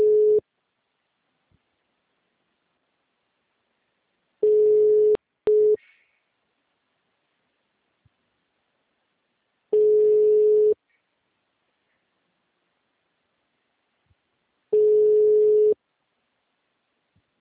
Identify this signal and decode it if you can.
Ringback tone (Europe, 425 Hz).
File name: out- -2010- - - .386